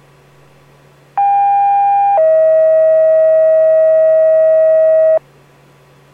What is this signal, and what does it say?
Two-tone page, heard as a one-second tone, then a three-second tone, about 800 then 640 Hz. It is heard from 1.2 to 5.2 s.